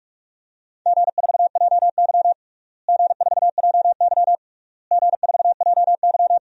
Morse code, heard as a keyed tone, about 42 wpm, 690 Hz. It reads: G4JY G4JY G4JY
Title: DXCC-100 Call Signs - 4 Characters - Repeated 3x - 45wpm